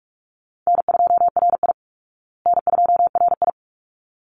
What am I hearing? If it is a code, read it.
Morse code, heard as a keyed tone, about 44 wpm, 700 Hz. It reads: N2RI N2RI
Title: Call Signs Repeated 3x - 45wpm